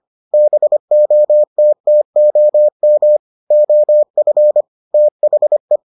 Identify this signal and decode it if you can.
Morse code, heard as a keyed tone, about 25 wpm, 600 Hz. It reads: BOTTOM OF THE